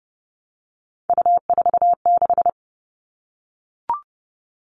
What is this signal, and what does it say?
Morse code, heard as a keyed tone, about 30 wpm, 700 Hz. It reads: U46
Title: All Cut Numbers - 3-Digit 30wpm